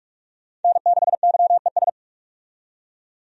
Morse code, heard as a keyed tone, about 46 wpm, 700 Hz. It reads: N6YES
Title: USA Call Signs - 5 Characters - 45wpm